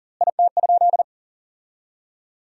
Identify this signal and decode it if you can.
Morse code, heard as a keyed tone, about 40 wpm, 700 Hz. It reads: IT?
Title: 2x Word-Spacing - Sentences from Top 100 Words - 40wpm